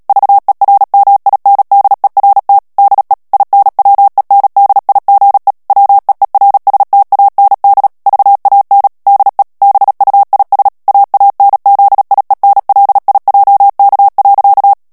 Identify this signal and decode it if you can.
Morse code, heard as a keyed tone, about 37 wpm, 800 Hz. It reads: VERMINDERT DE INWENDIGE WEERSTAND VAN DE BUIS AANZIENLIJK.
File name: morse mp3 / HOLLANDAIS texte clair 2 vitesse 35 mots minute